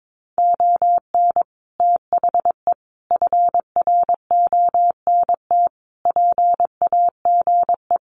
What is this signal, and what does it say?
Morse code, heard as a keyed tone, about 22 wpm, 700 Hz. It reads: ON THE FRONT PAGE
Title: Sets of 5 Words Encore 22wpm